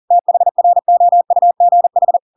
Morse code, heard as a keyed tone, about 40 wpm, 700 Hz. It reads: THROUGH
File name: Pinocchio / Pinocchio ch7 40 40 ik1hge